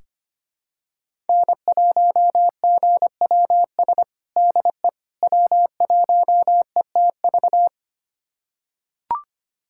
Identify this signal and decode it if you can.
Morse code, heard as a keyed tone, about 25 wpm, 700 Hz. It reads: N1GWS DE W1ETV
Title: USA Callsign DE Callsign - 5 Characters - 25wpm